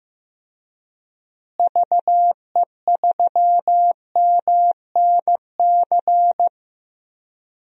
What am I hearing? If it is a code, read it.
Morse code, heard as a keyed tone, about 15 wpm, 700 Hz. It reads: VE3MNC
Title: DXCC-100 Call Signs - 6 Characters - Repeated 3x - 15wpm